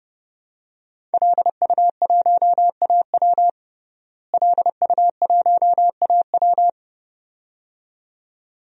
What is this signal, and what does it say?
Morse code, heard as a keyed tone, about 32 wpm, 700 Hz. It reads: LU1AW LU1AW
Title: DXCC-100 Call Signs - 5 Characters - Repeated 2x - 30wpm